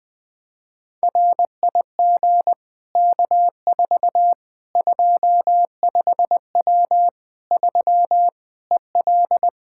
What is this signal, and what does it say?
Morse code, heard as a keyed tone, about 20 wpm, 700 Hz. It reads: RIG K4 25W 3 EL
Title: Practice QSO - Exchange 2 - Info - 20wpm